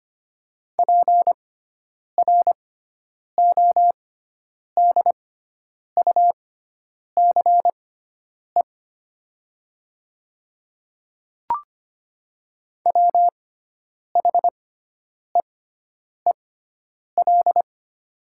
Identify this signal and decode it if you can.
Morse code, heard as a keyed tone, about 25 wpm, spaced as for 9 wpm, 700 Hz. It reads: PRODUCE WHEEL
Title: 6x Character Spacing - Top 500 Words - 25wpm